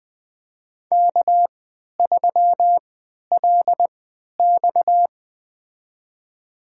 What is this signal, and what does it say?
Morse code, heard as a keyed tone, about 21 wpm, spaced as for 11 wpm, 700 Hz. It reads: K3LX
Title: USA Call Signs - 4 Characters - 3x Character Spacing - 20wpm